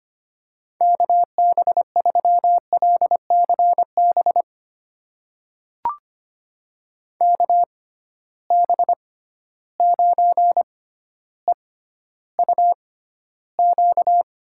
Morse code, heard as a keyed tone, about 25 wpm, spaced as for 13 wpm, 700 Hz. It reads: KB3LCB KB9EUQ
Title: USA Call Signs - 6 Characters - 6x Character Spacing - 25wpm